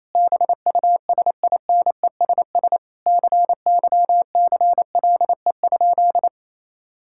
Morse code, heard as a keyed tone, about 28 wpm, 700 Hz. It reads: BUSINESS CYCLE?
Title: Sentences from Top 1000 Words - Review Part 6 - 28wpm